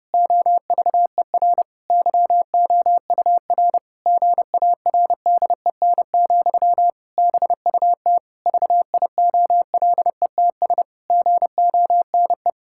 Morse code, heard as a keyed tone, about 30 wpm, 700 Hz. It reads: OVER YOUR GARDEN, BUT VIOLETS GONE
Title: Pangram Challenge v2 - 30wpm